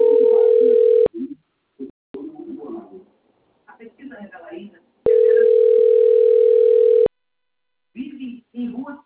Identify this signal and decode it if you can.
Ringback tone (North America), 440 + 480 Hz, with speech.
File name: rg-1600-unknown- - - .77745